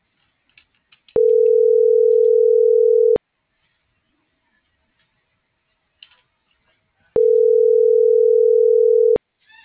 Ringback tone (North America), 440 + 480 Hz.